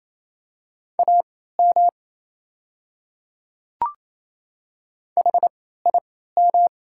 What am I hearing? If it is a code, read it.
Morse code, heard as a keyed tone, about 28 wpm, spaced as for 16 wpm, 700 Hz. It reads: AM HIM
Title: 3x Character Spacing - Top 500 Words - 28wpm